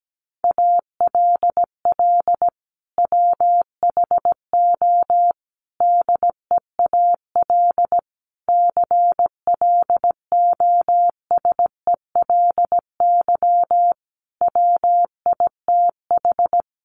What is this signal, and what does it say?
Morse code, heard as a keyed tone, about 17 wpm, 700 Hz. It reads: ALL WHO DEAL CLOSELY WITH
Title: Sentences from Top 1400 Words 17wpm